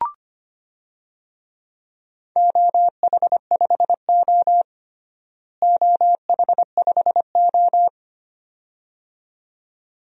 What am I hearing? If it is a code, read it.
Morse code, heard as a keyed tone, about 25 wpm, 700 Hz. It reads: OH5O OH5O
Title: DXCC-100 Call Signs - 4 Characters - Repeated 2x - 25wpm